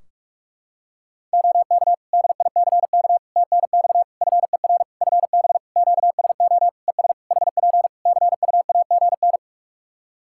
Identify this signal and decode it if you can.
Morse code, heard as a keyed tone, about 46 wpm, 700 Hz. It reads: OK DICK TNX FER FB QSO ES HP CUAGN
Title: Practice QSO - Exchange 4 - Ending - 45wpm